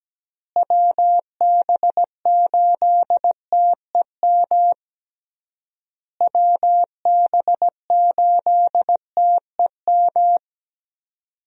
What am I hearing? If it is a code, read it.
Morse code, heard as a keyed tone, about 17 wpm, 700 Hz. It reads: WB8TEM WB8TEM
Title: USA Call Signs - 6 Characters - Repeated 3x - 17wpm